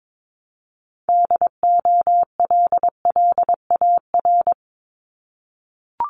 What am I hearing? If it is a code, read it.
Morse code, heard as a keyed tone, about 22 wpm, 700 Hz. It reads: DOLLAR
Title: Multi-letter Phonemes Repeated 3x 22wpm